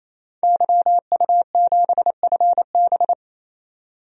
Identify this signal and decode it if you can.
Morse code, heard as a keyed tone, about 28 wpm, 700 Hz. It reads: YU7FB
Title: DXCC-100 Call Signs - 5 Characters - 28wpm